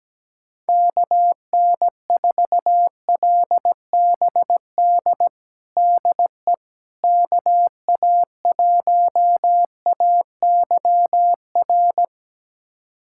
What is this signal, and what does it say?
Morse code, heard as a keyed tone, about 17 wpm, 700 Hz. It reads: KN4LBD DE KA1AYR
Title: USA Callsign DE Callsign - 6 Characters - 17wpm